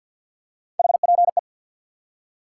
Morse code, heard as a keyed tone, about 51 wpm, 700 Hz. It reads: HPE
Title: Top 90 QSO Elements 50wpm